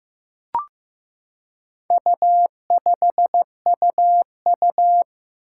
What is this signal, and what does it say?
Morse code, heard as a keyed tone, about 15 wpm, 700 Hz. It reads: U5UU
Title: Four Challenging Letter-Number Combinations 15wpm